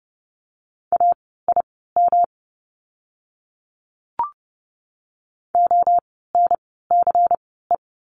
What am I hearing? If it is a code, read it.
Morse code, heard as a keyed tone, about 30 wpm, spaced as for 17 wpm, 700 Hz. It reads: AIM ONCE
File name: course / 30:10wpm / 108 Lesson 18f - Letter m - Words 30wpm